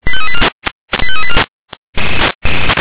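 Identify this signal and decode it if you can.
Five-tone selective call (ZVEI), heard as quick runs of five tones, tones of about 70 ms. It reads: 85384
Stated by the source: ZVEI five-tone selective call received on an SDR.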